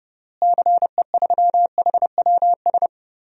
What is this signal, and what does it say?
Morse code, heard as a keyed tone, about 30 wpm, 700 Hz. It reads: CE3HWS
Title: DXCC-100 Call Signs - 6 Characters - Repeated 3x - 30wpm